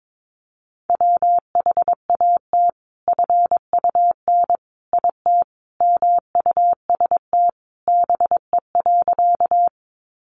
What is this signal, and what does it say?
Morse code, heard as a keyed tone, about 22 wpm, 700 Hz. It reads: WHAT FUN IT MUST BE.
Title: Sentences from Top 700 Words 22wpm